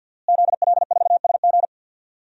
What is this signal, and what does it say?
Morse code, heard as a keyed tone, about 50 wpm, 700 Hz. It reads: ZL4SG
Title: DXCC-100 Call Signs - 5 Characters - Repeated 3x - 50wpm